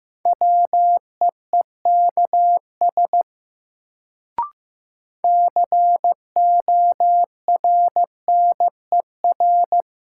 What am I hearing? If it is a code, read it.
Morse code, heard as a keyed tone, about 15 wpm, 700 Hz. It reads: WEEKS CORNER
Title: Sets of 5 Words - New Word Drill - 15wpm